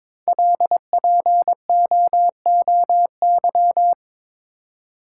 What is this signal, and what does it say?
Morse code, heard as a keyed tone, about 22 wpm, 690 Hz. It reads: LPOOY
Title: Code Groups - 5 Letters - 22wpm